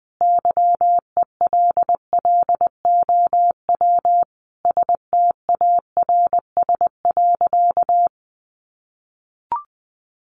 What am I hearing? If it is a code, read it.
Morse code, heard as a keyed tone, about 19 wpm, 700 Hz. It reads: YELLOW STARS.
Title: Additional Easy Phrases 20wpm